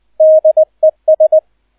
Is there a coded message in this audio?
Yes: Morse code from 0.2 to 1.4 s, heard as a keyed tone, about 20 wpm, 630 Hz. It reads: DES